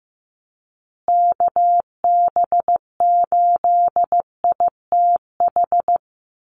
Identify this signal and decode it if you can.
Morse code, heard as a keyed tone, about 15 wpm, 700 Hz. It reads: KB8ITH
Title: USA Call Signs Repeated 2x 15wpm